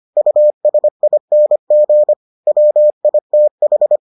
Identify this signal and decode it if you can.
Morse code, heard as a keyed tone, about 25 wpm, 600 Hz. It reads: USING WITH